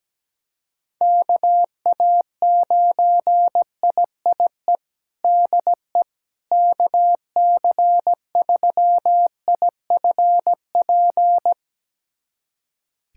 Morse code, heard as a keyed tone, about 17 wpm, 700 Hz. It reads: KA9IIE DE KC3IFP
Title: USA Callsign DE Callsign 17wpm